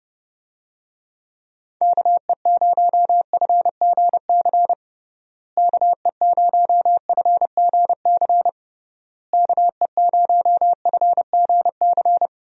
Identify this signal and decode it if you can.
Morse code, heard as a keyed tone, about 30 wpm, 700 Hz. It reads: KE0FGC KE0FGC KE0FGC
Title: USA Call Signs - 6 Characters - Repeated 3x - 30wpm